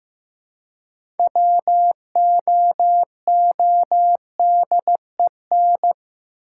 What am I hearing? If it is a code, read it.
Morse code, heard as a keyed tone, about 15 wpm, 700 Hz. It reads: WOODEN
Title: Six Letter Words - 15wpm